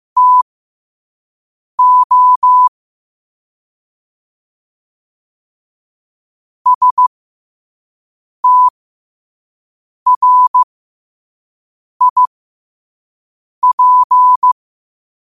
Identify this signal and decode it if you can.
Morse code, heard as a keyed tone, about 15 wpm, spaced as for 5 wpm, 1000 Hz. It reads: TO STRIP
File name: qotdfiles / 5wpm